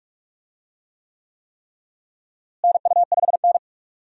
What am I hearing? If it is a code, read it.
Morse code, heard as a keyed tone, about 46 wpm, 700 Hz. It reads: NU5N
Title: USA Call Signs - 4 Characters - Repeated 3x - 45wpm